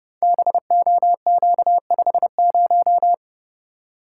Morse code, heard as a keyed tone, about 29 wpm, 700 Hz. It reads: BOQ50
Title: Code Groups - 5 Letter:Numbers - 30wpm